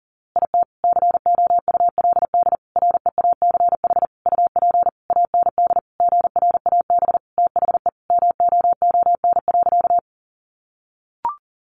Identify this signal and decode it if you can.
Morse code, heard as a keyed tone, about 40 wpm, 700 Hz. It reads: IT COULD REACH UP AND GRAB THE MOON.
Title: A Sound Of Thunder 40wpm